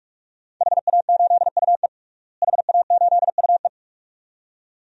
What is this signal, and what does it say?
Morse code, heard as a keyed tone, about 46 wpm, 700 Hz. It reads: HA8UE HA8UE
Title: DXCC-100 Call Signs - 5 Characters - Repeated 2x - 45wpm